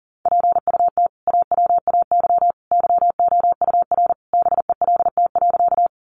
Morse code, heard as a keyed tone, about 40 wpm, 700 Hz. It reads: PUT AWAY YOUR BELT.
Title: Sentences from Top 1400 Words 40wpm